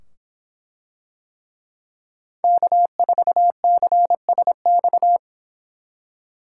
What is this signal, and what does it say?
Morse code, heard as a keyed tone, about 25 wpm, 700 Hz. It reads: K4CSX